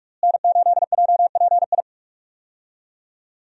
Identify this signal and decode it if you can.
Morse code, heard as a keyed tone, about 44 wpm, 700 Hz. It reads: N8JPI
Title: USA Call Signs - 5 Characters - Repeated 2x - 45wpm